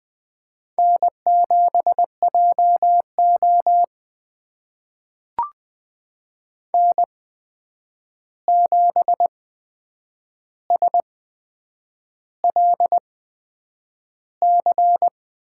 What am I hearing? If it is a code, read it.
Morse code, heard as a keyed tone, about 20 wpm, spaced as for 8 wpm, 700 Hz. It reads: N7JO N7SLC
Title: USA Call Signs - 8x Character Spacing - 20wpm